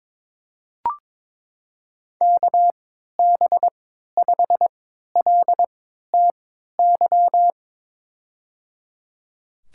Morse code, heard as a keyed tone, about 22 wpm, spaced as for 13 wpm, 700 Hz. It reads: KB5LTY
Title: USA Call Signs - 3x Character Spacing - 22wpm